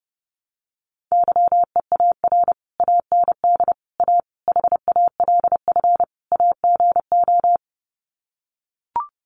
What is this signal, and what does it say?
Morse code, heard as a keyed tone, about 29 wpm, 700 Hz. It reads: YEAR AND A HALF AGO
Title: Sets of 5 Words Encore 30wpm